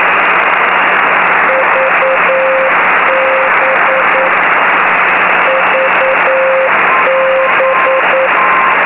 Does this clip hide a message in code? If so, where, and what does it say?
Yes: Morse code from 1.5 to 8.3 s, heard as a keyed tone, about 9 wpm, 530 Hz, with noise. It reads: VB VB